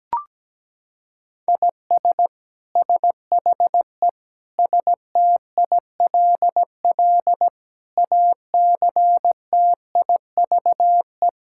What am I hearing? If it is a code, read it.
Morse code, heard as a keyed tone, about 17 wpm, 700 Hz. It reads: IS SHE STILL ACTIVE
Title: Sentences from Top 1000 Words - Review Part 6 - 17wpm